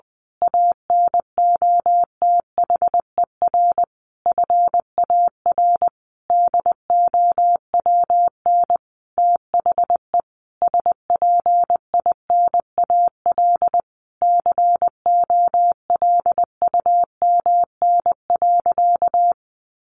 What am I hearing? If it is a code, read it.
Morse code, heard as a keyed tone, about 20 wpm, 700 Hz. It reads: ANOTHER FAR DOWN THE SPINAL COLUMN.
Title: A Sound Of Thunder 20wpm